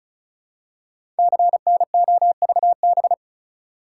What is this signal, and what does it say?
Morse code, heard as a keyed tone, about 36 wpm, 700 Hz. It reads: CNOVB
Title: Code Groups - 5 Letters - 35wpm